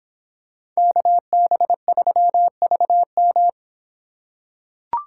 Morse code, heard as a keyed tone, about 24 wpm, 700 Hz. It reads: KB3VM